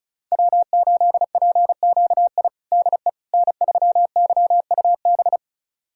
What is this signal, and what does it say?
Morse code, heard as a keyed tone, about 36 wpm, 690 Hz. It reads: W8PQI DE N3YUB
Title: USA Callsign DE Callsign - 5 Characters - 35wpm